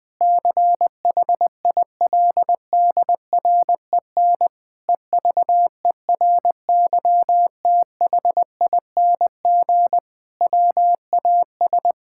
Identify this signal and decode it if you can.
Morse code, heard as a keyed tone, about 20 wpm, 700 Hz. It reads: CHILDREN EVERYTHING WAS
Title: Sentences with 5-Word Phrases - Spoken Beginning - 20wpm